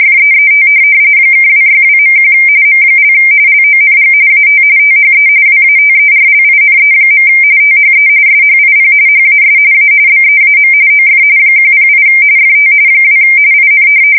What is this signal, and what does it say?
RTTY, about 45.45 baud. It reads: FEET (262.1 M) LONG AT THE WATERLINE. THE SHIP HAS A BEAM OF 108 FT 2 IN
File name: fsk170 / USS-Missouri